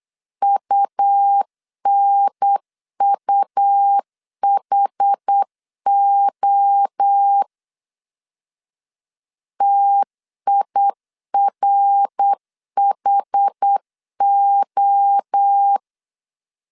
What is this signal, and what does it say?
Morse code, heard as a keyed tone, about 8 wpm, 800 Hz. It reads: UNUHO TIRHO
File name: CW / DET / 1 3 1500C S B 8wpm 4141